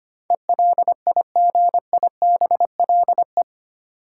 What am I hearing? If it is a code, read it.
Morse code, heard as a keyed tone, about 25 wpm, 700 Hz. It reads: ELIGIBLE
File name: 8 Letter Words - First 3 Letters Given - 25wpm